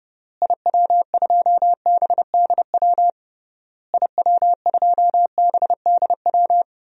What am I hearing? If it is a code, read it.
Morse code, heard as a keyed tone, about 30 wpm, 700 Hz. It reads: IW2BDW IW2BDW
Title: DXCC-100 Call Signs - 6 Characters - Repeated 3x - 30wpm